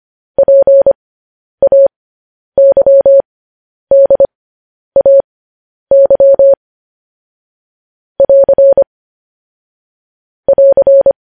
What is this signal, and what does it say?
Morse code, heard as a keyed tone, about 25 wpm, spaced as for 10 wpm, 550 Hz. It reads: PAYDAY <AR> <AR>
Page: news-headlines-in-morse-code-at-10-wpm